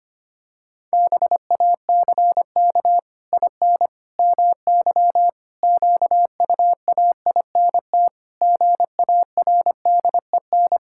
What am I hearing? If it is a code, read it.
Morse code, heard as a keyed tone, about 25 wpm, 700 Hz. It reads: BACK IN MY QUAINT GARDEN
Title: Pangram Challenge v2 - 25wpm